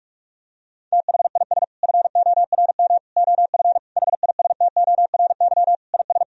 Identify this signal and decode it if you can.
Morse code, heard as a keyed tone, about 41 wpm, 700 Hz. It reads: THIS FORM OF HISTORY IS
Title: Sentences from Top 1000 Words - Review Part 4 - 45wpm